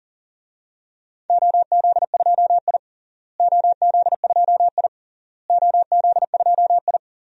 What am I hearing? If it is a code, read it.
Morse code, heard as a keyed tone, about 40 wpm, 700 Hz. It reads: OZ2I OZ2I OZ2I
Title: DXCC-100 Call Signs - 4 Characters - Repeated 3x - 40wpm